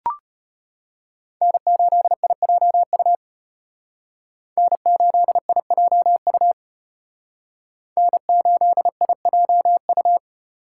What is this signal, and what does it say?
Morse code, heard as a keyed tone, about 32 wpm, 700 Hz. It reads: N8IJU N8IJU N8IJU